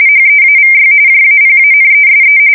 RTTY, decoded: HEAVY JAPANESE

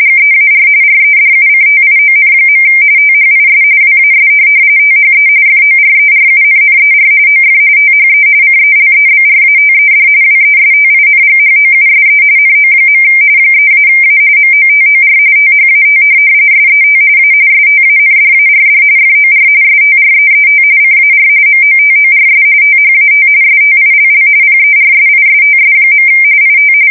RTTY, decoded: GEAR REDUCTION. THIS WAS THE FORERUNNER OF THE CELEBRATED WESTINGHOUSE NO. 3-A MOTOR THAT SET A STANDARD FOR STREET CAR MOTORS THAT STILL PERSISTS. IT

